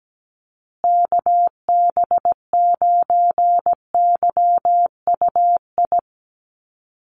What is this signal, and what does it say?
Morse code, heard as a keyed tone, about 17 wpm, 700 Hz. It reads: KB9YUI
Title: USA Call Signs Repeated 3x 17wpm